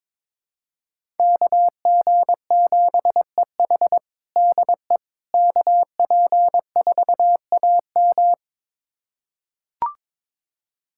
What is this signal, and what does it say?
Morse code, heard as a keyed tone, about 22 wpm, 700 Hz. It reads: KG7EH DE KP4AM
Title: USA Callsign DE Callsign - 5 Characters - 22wpm